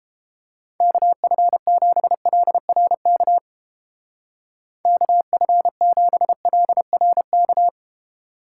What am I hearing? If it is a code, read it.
Morse code, heard as a keyed tone, about 29 wpm, 700 Hz. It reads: KF7LRK KF7LRK